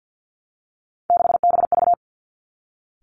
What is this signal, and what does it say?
Morse code, heard as a keyed tone, about 50 wpm, 700 Hz. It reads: 6BV
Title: Three Challenging Letter-Number Combinations 50wpm